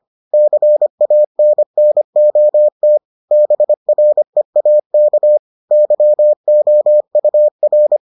Morse code, heard as a keyed tone, about 25 wpm, 600 Hz. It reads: CANNOT BREAK YOUR